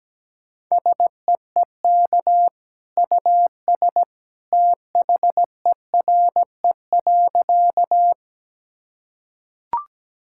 Morse code, heard as a keyed tone, about 17 wpm, 700 Hz. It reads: SEEK US THERE.
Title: Sentences from Top 1600 Words 17wpm